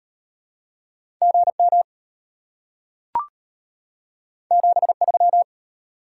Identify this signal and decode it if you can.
Morse code, heard as a keyed tone, about 32 wpm, 700 Hz. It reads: GM 73